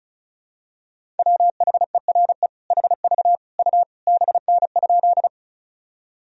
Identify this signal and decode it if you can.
Morse code, heard as a keyed tone, about 34 wpm, 700 Hz. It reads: WHERE HV U BN?
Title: Sentences from Top 200 Words and Abbreviations 35wpm